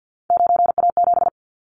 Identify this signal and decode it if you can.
Morse code, heard as a keyed tone, about 50 wpm, 700 Hz. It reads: 9A7
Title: Callsign Prefixes for Top 50 DXCC - 50wpm